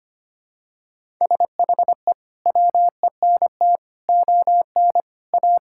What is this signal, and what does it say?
Morse code, heard as a keyed tone, about 25 wpm, 700 Hz. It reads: SHE WENT ON A
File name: Sentences with 2-Word Phrases - 25wpm